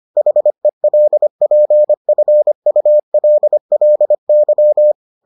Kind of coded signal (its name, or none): Morse code